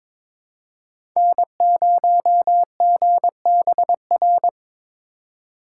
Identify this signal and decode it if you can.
Morse code, heard as a keyed tone, about 22 wpm, 700 Hz. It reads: N0GBR
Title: USA Call Signs - 5 Characters - 2x Character Spacing - 22wpm